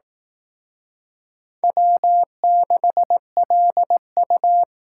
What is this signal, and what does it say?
Morse code, heard as a keyed tone, about 19 wpm, 700 Hz. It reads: W6LU